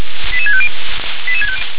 Five-tone selective call (ZVEI), heard as quick runs of five tones, tones of about 70 ms. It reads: 88650
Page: SDR/ZVEI